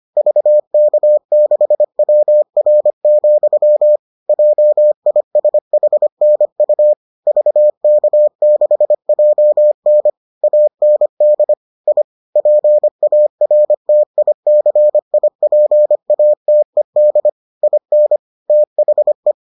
Morse code, heard as a keyed tone, about 25 wpm, 600 Hz. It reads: VK6WR, JISHNU VK6JN AND I PARTICIPATED IN THE